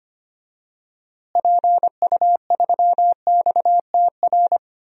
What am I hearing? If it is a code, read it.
Morse code, heard as a keyed tone, about 25 wpm, 700 Hz. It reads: PU3XTR
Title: DXCC-100 Call Signs - 6 Characters - Repeated 2x - 25wpm